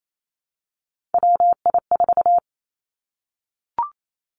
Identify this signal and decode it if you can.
Morse code, heard as a keyed tone, about 28 wpm, 700 Hz. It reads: WI4
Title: Three Challenging Letter-Number Combinations 28wpm